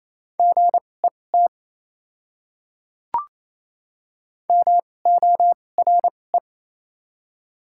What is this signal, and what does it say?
Morse code, heard as a keyed tone, about 28 wpm, spaced as for 20 wpm, 700 Hz. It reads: GET MORE
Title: 2x Character Spacing - Top 200 Words - 28wpm